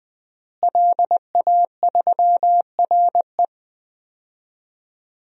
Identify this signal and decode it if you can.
Morse code, heard as a keyed tone, about 21 wpm, 700 Hz. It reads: LA3RE
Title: DXCC-100 Call Signs - 5 Characters - Repeated 2x - 20wpm